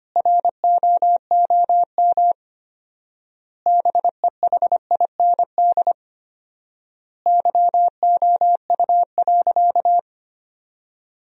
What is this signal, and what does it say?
Morse code, heard as a keyed tone, about 25 wpm, 700 Hz. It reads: ROOM BEHIND YOU.
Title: 4x Word-Spacing - Sentences from Top 500 Words - 25wpm